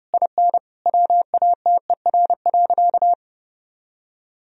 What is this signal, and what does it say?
Morse code, heard as a keyed tone, about 30 wpm, 700 Hz. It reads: IN WATER.
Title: Sentences from Top 1000 Words - Review Part 1 - 30wpm